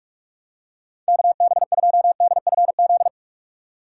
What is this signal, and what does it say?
Morse code, heard as a keyed tone, about 42 wpm, 700 Hz. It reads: KB2DFZ
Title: USA Call Signs - 4x Character Spacing - 45wpm